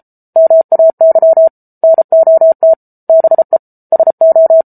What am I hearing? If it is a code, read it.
Morse code, heard as a keyed tone, about 34 wpm, 650 Hz. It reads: MAY NOT BE SO